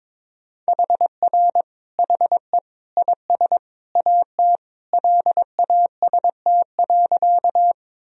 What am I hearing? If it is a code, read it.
Morse code, heard as a keyed tone, about 22 wpm, 700 Hz. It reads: HR HE IS AT LAST.
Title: Sentences from Top 300 Words and Abbreviations 22wpm